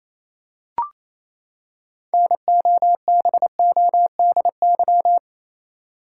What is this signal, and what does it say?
Morse code, heard as a keyed tone, about 28 wpm, 700 Hz. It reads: NOBODY
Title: Six Letter Words - 28wpm